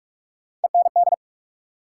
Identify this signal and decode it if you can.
Morse code, heard as a keyed tone, about 46 wpm, 700 Hz. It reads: END
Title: Sentences from Top 200 Words - New Word Drill - 45wpm